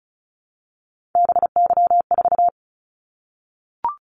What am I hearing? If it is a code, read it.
Morse code, heard as a keyed tone, about 36 wpm, 700 Hz. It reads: BY4
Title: Callsign Prefixes for Top 100 DXCC - 35wpm